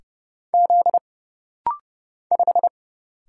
Morse code, heard as a keyed tone, about 29 wpm, 700 Hz. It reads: Z 5
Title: Single Letter-Number Alt - Rapid-Fire 30wpm